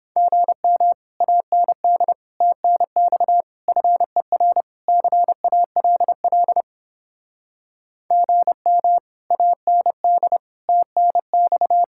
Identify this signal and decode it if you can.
Morse code, heard as a keyed tone, about 29 wpm, 700 Hz. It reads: GM AND TNX FER CALL GM AND TNX